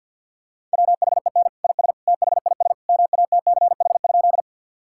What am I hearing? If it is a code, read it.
Morse code, heard as a keyed tone, about 51 wpm, 700 Hz. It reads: WHEN IS THIS MATCH?
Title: Sentences from Top 1600 Words 50wpm